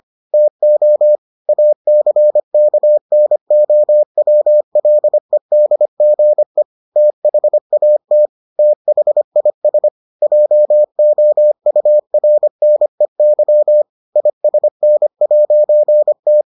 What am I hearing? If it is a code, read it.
Morse code, heard as a keyed tone, about 25 wpm, 600 Hz. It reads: TO ACKNOWLEDGE THAT THIS JOURNEY ISN'T